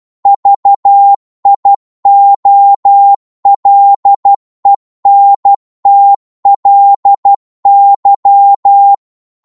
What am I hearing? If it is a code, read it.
Morse code, heard as a keyed tone, about 12 wpm, 800 Hz. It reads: VIOLENTLY